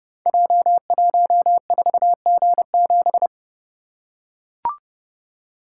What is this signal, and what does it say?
Morse code, heard as a keyed tone, about 31 wpm, 700 Hz. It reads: J14G7
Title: Code Groups - 5 Letter:Numbers - 30wpm